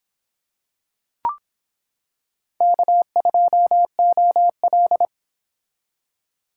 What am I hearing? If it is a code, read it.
Morse code, heard as a keyed tone, about 26 wpm, 700 Hz. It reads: K2OL